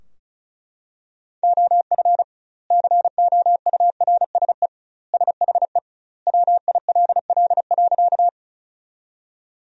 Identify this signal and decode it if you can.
Morse code, heard as a keyed tone, about 34 wpm, 700 Hz. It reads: OF COURSE SHE WILL.
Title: 2x Word-Spacing - Sentences from Top 500 Words - 35wpm